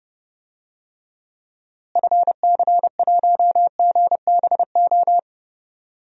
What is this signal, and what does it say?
Morse code, heard as a keyed tone, about 32 wpm, 700 Hz. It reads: FC1GBO
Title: DXCC-100 Call Signs - 6 Characters - Repeated 2x - 30wpm